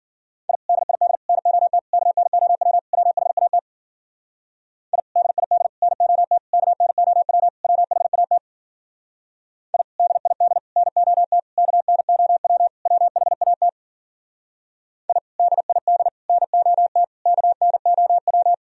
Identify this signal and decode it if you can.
Morse code, heard as a keyed tone, about 51 wpm, 700 Hz. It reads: I DID NOT KNOW WHAT I DID NOT KNOW WHAT I DID NOT KNOW WHAT I DID NOT KNOW